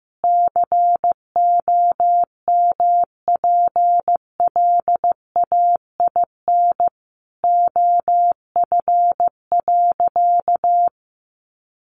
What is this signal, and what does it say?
Morse code, heard as a keyed tone, about 15 wpm, 700 Hz. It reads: COMPLAIN OF.
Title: Sentences from Top 1300 Words 15wpm